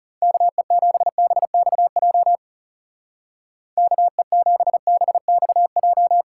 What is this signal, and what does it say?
Morse code, heard as a keyed tone, about 37 wpm, 700 Hz. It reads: KE7BXJ KE7BXJ
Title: Speed Racing - USA Call Signs 30wpm